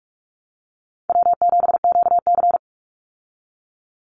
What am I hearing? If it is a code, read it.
Morse code, heard as a keyed tone, about 44 wpm, 700 Hz. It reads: W7QC
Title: Call Signs Repeated 2x - 45wpm